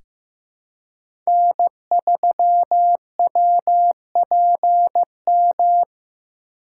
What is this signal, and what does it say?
Morse code, heard as a keyed tone, about 15 wpm, 700 Hz. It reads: N3WPM
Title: USA Call Signs - 5 Characters - 8x Character Spacing - 15wpm